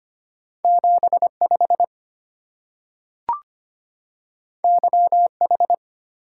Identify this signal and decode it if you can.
Morse code, heard as a keyed tone, about 25 wpm, 700 Hz. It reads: 75 YH
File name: Two Letter-Number Combinations - 25wpm